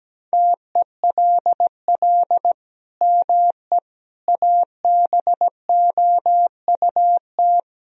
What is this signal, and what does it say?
Morse code, heard as a keyed tone, about 17 wpm, 700 Hz. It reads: TELL ME ABOUT
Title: Sentences from Top 1000 Words - Review Part 3 - 17wpm